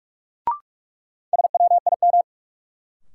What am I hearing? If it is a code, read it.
Morse code, heard as a keyed tone, about 46 wpm, 700 Hz. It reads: SWIM
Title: Sentences from Top 1000 Words - Review Part 6 - New Word Drill - 45wpm